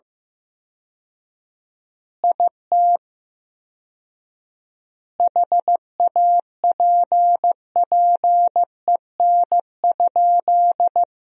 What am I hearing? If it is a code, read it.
Morse code, heard as a keyed tone, about 15 wpm, 700 Hz. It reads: IT HAPPEN?
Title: 4x Word-Spacing - Sentences from Top 400 Words - 15wpm